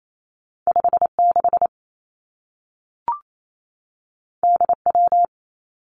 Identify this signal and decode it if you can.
Morse code, heard as a keyed tone, about 28 wpm, 700 Hz. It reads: 5 6 D W
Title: Two Challenging Letter-Number Combinations 28wpm